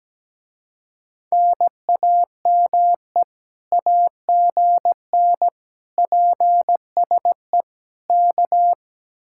Morse code, heard as a keyed tone, about 17 wpm, 700 Hz. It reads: NAME AGN PSE K
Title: Speed-Racing Common QSO Phrases Encore - 17wpm